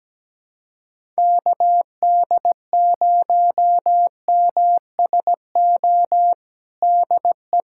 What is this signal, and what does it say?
Morse code, heard as a keyed tone, about 17 wpm, 700 Hz. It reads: KD0MSO DE
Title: USA Callsign DE Callsign - 6 Characters - 17wpm